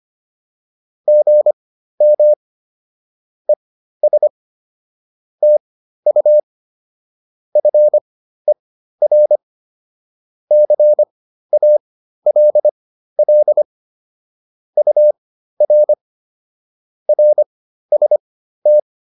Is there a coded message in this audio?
Yes: Morse code from 1.1 to 18.8 s, heard as a keyed tone, about 25 wpm, spaced as for 13 wpm, 600 Hz. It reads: GM ES TU FER CALL UR RST